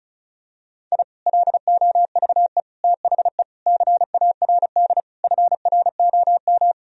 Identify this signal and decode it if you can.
Morse code, heard as a keyed tone, about 34 wpm, 700 Hz. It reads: I LOVE THE CARD FROM
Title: Sentences from Top 1000 Words - Review Part 4 - 35wpm